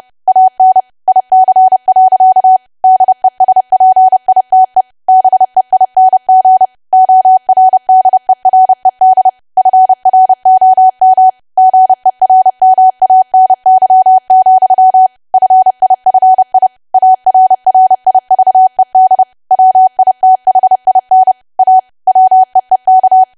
Morse code, heard as a keyed tone, about 30 wpm, 750 Hz. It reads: AN IC. DESPITE BEING ORDERED FROM GERMANY, FIFI ARRIVED WITHIN A WEEK
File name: cwa / W1AW-Level3 / 30WPM